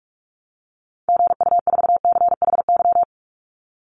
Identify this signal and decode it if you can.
Morse code, heard as a keyed tone, about 44 wpm, 700 Hz. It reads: GU4CHY
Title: Call Signs Repeated 3x - 45wpm